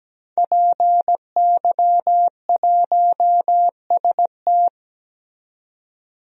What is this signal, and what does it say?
Morse code, heard as a keyed tone, about 17 wpm, 700 Hz. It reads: PY1ST
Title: DXCC-100 Call Signs - 5 Characters - 17wpm